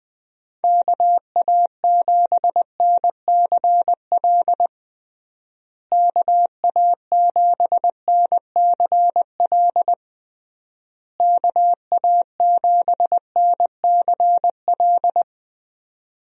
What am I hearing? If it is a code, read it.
Morse code, heard as a keyed tone, about 20 wpm, 700 Hz. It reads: KA7NCL KA7NCL KA7NCL
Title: USA Call Signs - 6 Characters - Repeated 3x - 20wpm